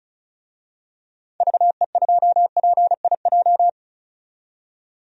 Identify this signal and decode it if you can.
Morse code, heard as a keyed tone, about 34 wpm, 700 Hz. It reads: VE2PIJ
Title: DXCC-100 Call Signs - 6 Characters - 35wpm